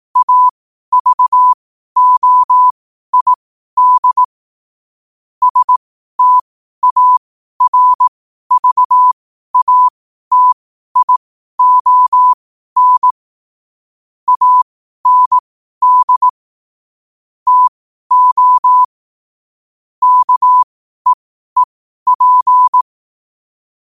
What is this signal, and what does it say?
Morse code, heard as a keyed tone, about 18 wpm, spaced as for 12 wpm, 1000 Hz. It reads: AVOID STARVATION AND TO KEEP
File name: qotdfiles / 12wpm